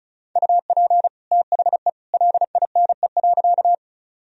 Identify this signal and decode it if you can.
Morse code, heard as a keyed tone, about 36 wpm, 700 Hz. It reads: UP THE LINE.
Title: Sentences from Top 1000 Words - Review Part 1 - 35wpm